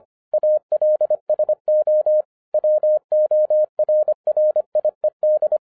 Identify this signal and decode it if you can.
Morse code, heard as a keyed tone, about 25 wpm, 610 Hz. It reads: ALSO WORRIED
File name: 2011-07-mp3 / n34 THE 3.11 2 25wpm